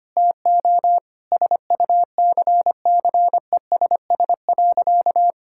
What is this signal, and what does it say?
Morse code, heard as a keyed tone, about 25 wpm, 700 Hz. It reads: TO SUCCESS.
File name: practice / 25wpm / Difficult Sentences with Hints - 25wpm